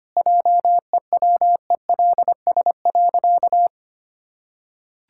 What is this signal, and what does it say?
Morse code, heard as a keyed tone, about 25 wpm, 700 Hz. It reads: JEWELS.
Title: Pangram Challenge v2 - 25wpm